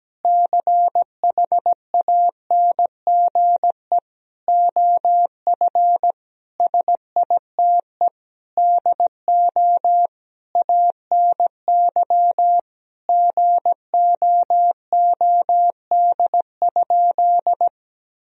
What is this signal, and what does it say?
Morse code, heard as a keyed tone, about 17 wpm, 700 Hz. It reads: CHANGE OF SITE DO ANY GOOD?
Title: Sentences from Top 1000 Words - Review Part 4 - 17wpm